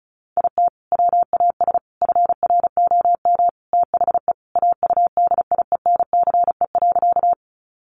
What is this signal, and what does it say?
Morse code, heard as a keyed tone, about 34 wpm, 700 Hz. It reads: IT WAS FROM THE AUDIENCE.
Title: Sentences from Top 800 Words 35wpm